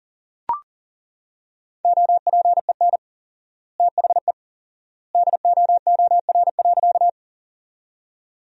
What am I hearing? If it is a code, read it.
Morse code, heard as a keyed tone, about 40 wpm, 700 Hz. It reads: OPEN THE DOOR.
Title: 4x Word-Spacing - Sentences from Top 300 Words - 40wpm